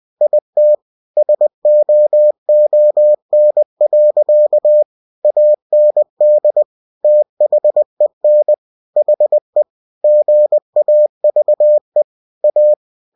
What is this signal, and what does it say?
Morse code, heard as a keyed tone, about 20 wpm, 600 Hz. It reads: IT SOON. AND THEN HE GAVE A
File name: cw-books / winnie-the-pooh / winnie-20wpm-page-0093